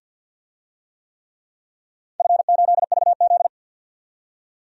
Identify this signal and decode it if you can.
Morse code, heard as a keyed tone, about 47 wpm, 700 Hz. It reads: F8VZ
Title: DXCC-100 Call Signs - 4 Characters - 50wpm